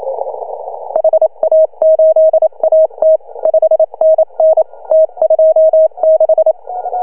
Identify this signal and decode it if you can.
Morse code, heard as a keyed tone, about 26 wpm, 620 Hz. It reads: HA8AT 5NN T26
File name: CW / HA2OS RHOB 2017 CW